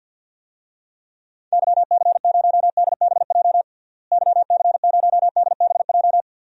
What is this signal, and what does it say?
Morse code, heard as a keyed tone, about 52 wpm, 700 Hz. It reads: YC0DBJ YC0DBJ
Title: DXCC-100 Call Signs - 6 Characters - Repeated 2x - 50wpm